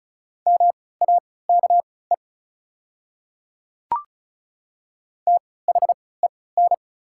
Morse code, heard as a keyed tone, about 36 wpm, spaced as for 20 wpm, 690 Hz. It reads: MAKE THEN
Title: 3x Character Spacing - Top 100 Words - 35wpm